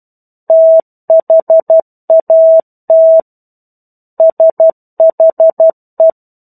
Morse code, heard as a keyed tone, about 12 wpm, 650 Hz. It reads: THAT SHE
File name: 01-Chapter-01 12 WPM